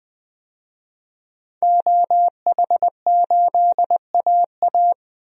Morse code, heard as a keyed tone, about 20 wpm, 700 Hz. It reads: OH8AA
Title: DXCC-100 Call Signs - 5 Characters - 20wpm